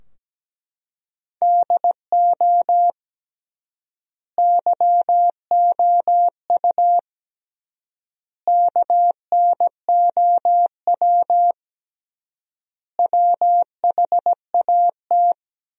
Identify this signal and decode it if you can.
Morse code, heard as a keyed tone, about 17 wpm, 700 Hz. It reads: DO YOU KNOW WHAT
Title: 3x Word-Spacing - Sentences from Top 300 Words - 17wpm